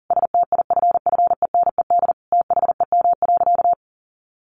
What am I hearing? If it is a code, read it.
Morse code, heard as a keyed tone, about 40 wpm, 700 Hz. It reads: STIFFENED THEM.
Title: A Sound Of Thunder 40wpm